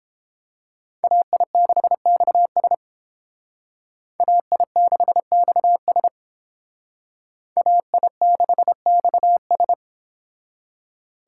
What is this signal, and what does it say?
Morse code, heard as a keyed tone, about 30 wpm, 700 Hz. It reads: AI6XS AI6XS AI6XS